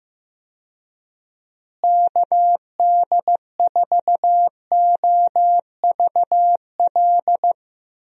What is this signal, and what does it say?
Morse code, heard as a keyed tone, about 15 wpm, 700 Hz. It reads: KD4OVL
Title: USA Call Signs - 6 Characters - Repeated 3x - 15wpm